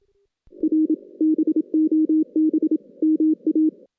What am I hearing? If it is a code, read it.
Morse code, heard as a keyed tone, about 27 wpm, 310 Hz. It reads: R BOB MA
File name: 2016 JAN CW NAQP / 2016 Jan naqp k5ot - 0-4